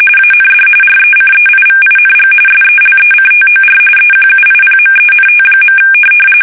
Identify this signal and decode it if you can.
RTTY, about 45.45 baud. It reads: AND RECEIVING SELECTOR EQUIPMENT.
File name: fsk850 / teletype-corp